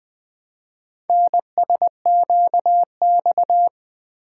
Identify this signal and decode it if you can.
Morse code, heard as a keyed tone, about 20 wpm, 700 Hz. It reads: NSQX
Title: Four Letter-Number Combinations - 20wpm